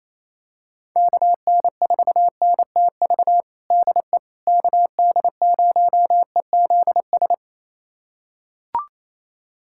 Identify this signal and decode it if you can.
Morse code, heard as a keyed tone, about 28 wpm, 700 Hz. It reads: KN4NTV DE KD0EZS
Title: USA Callsign DE Callsign 28wpm